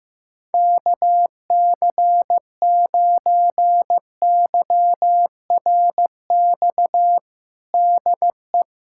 Morse code, heard as a keyed tone, about 15 wpm, 700 Hz. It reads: KC9YRX DE
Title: USA Callsign DE Callsign 15wpm